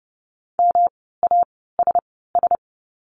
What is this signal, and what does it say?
Morse code, heard as a keyed tone, about 30 wpm, spaced as for 17 wpm, 700 Hz. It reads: MASS
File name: course / 30:10wpm / 108 Lesson 18f - Letter m - Words 30wpm